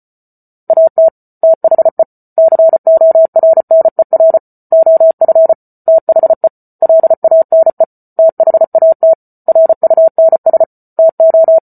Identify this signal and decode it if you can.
Morse code, heard as a keyed tone, about 34 wpm, 650 Hz. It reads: AT THE CORNER OF THE LANE THAT RUNS TO